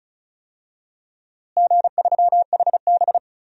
Morse code, heard as a keyed tone, about 34 wpm, 700 Hz. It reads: G3HB
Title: DXCC-100 Call Signs - 4 Characters - Repeated 2x - 35wpm